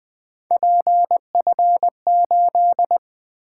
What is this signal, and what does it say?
Morse code, heard as a keyed tone, about 20 wpm, 700 Hz. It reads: PF8
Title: Three Letter-Number Combinations - 20wpm